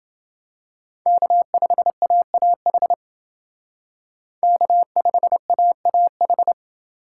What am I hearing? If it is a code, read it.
Morse code, heard as a keyed tone, about 29 wpm, 700 Hz. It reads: K5AAH K5AAH